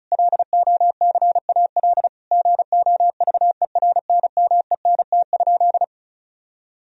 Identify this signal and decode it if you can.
Morse code, heard as a keyed tone, about 34 wpm, 700 Hz. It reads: LOCAL GOVERNMENT?
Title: Sentences from Top 1000 Words - Review Part 4 - 35wpm